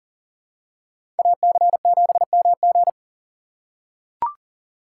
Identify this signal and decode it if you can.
Morse code, heard as a keyed tone, about 40 wpm, 700 Hz. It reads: AC7MG
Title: USA Call Signs - 5 Characters - 6x Character Spacing - 40wpm